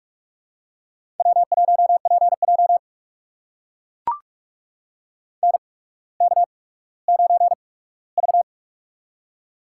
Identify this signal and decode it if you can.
Morse code, heard as a keyed tone, about 44 wpm, spaced as for 19 wpm, 700 Hz. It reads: W1PJ NK9V
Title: USA Call Signs - 4 Characters - 8x Character Spacing - 45wpm